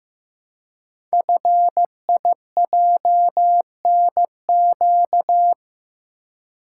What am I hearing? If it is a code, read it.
Morse code, heard as a keyed tone, about 15 wpm, 700 Hz. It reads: FIJNQ
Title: Code Groups - 5 Letters - 15wpm